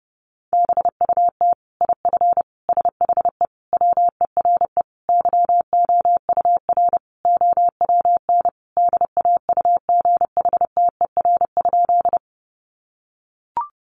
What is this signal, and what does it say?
Morse code, heard as a keyed tone, about 30 wpm, 700 Hz. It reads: BUT IF SHE WERE YOUR OWN DAUGHTER?
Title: Sentences from Top 1500 Words 30wpm